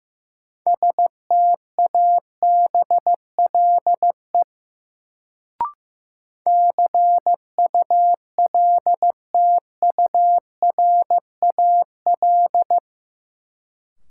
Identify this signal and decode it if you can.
Morse code, heard as a keyed tone, about 15 wpm, 700 Hz. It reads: STABLE CULTURAL
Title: Sentences from Top 1500 Words - New Word Drill Alt - 15wpm